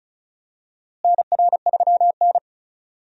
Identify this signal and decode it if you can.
Morse code, heard as a keyed tone, about 36 wpm, 700 Hz. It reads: NR3N
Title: USA Call Signs - 4 Characters - Repeated 2x - 35wpm